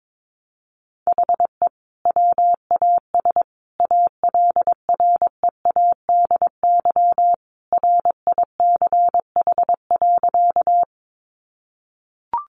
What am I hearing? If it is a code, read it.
Morse code, heard as a keyed tone, about 22 wpm, 700 Hz. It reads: HE WAS ALREADY RICH.
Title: Sentences from Top 1100 Words 22wpm